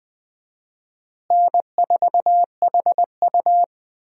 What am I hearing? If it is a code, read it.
Morse code, heard as a keyed tone, about 19 wpm, 700 Hz. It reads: N4HU
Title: USA Call Signs - 4 Characters - Repeated 3x - 20wpm